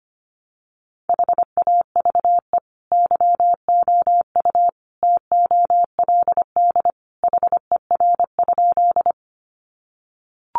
Morse code, heard as a keyed tone, about 25 wpm, 700 Hz. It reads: HAVE YOU TOLD HER?
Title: Sentences from Top 400 Words 25wpm